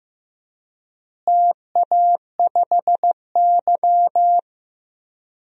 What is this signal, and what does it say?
Morse code, heard as a keyed tone, about 15 wpm, 700 Hz. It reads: TA5Y
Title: DXCC-100 Call Signs - 4 Characters - Repeated 3x - 15wpm